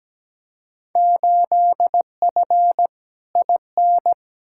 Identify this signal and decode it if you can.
Morse code, heard as a keyed tone, about 17 wpm, 700 Hz. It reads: 8F IN
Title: Field Day Exchanges Encore - 17wpm